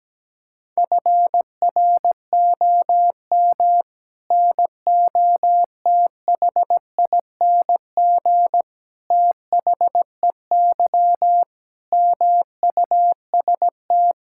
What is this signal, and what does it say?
Morse code, heard as a keyed tone, about 17 wpm, 700 Hz. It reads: FROM NOTHING THEY MUST
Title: Sentences from Top 1000 Words - Review Part 5 - 17wpm